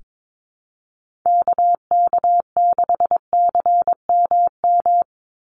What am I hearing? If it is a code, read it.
Morse code, heard as a keyed tone, about 22 wpm, 700 Hz. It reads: KK6CMM
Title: USA Call Signs 22wpm